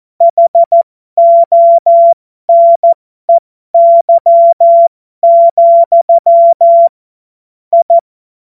Morse code, heard as a keyed tone, about 14 wpm, spaced as for 12 wpm, 680 Hz. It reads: HONEY, I
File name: Morsekurs / Witze